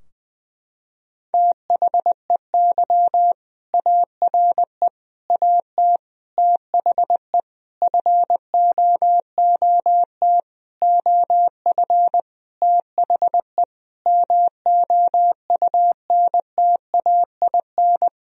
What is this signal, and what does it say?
Morse code, heard as a keyed tone, about 20 wpm, 700 Hz. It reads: THEY ARE AT THE FOOT OF THE MOUNTAIN
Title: Sentences from Top 1000 Words - Review Part 4 - 20wpm